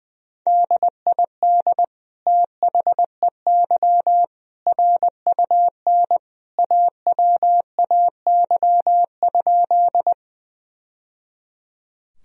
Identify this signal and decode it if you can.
Morse code, heard as a keyed tone, about 20 wpm, 700 Hz. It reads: DID THEY RUN AWAY?
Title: Sentences from Top 1000 Words - Review Part 1 - 20wpm